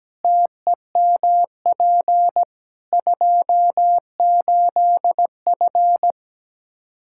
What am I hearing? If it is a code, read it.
Morse code, heard as a keyed tone, about 17 wpm, 700 Hz. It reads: TEMP 28F
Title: Practice QSO - Exchange 2 - Info - 17wpm